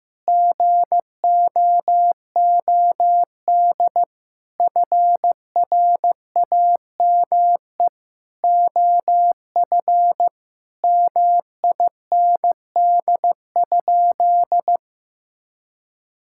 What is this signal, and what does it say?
Morse code, heard as a keyed tone, about 15 wpm, 700 Hz. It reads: GOOD FRAME OF MIND?
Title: Sentences from Top 1000 Words - Review Part 6 - 15wpm